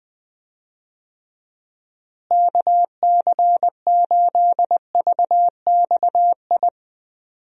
Morse code, heard as a keyed tone, about 21 wpm, 700 Hz. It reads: KC8VXI
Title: USA Call Signs - 6 Characters - Repeated 3x - 20wpm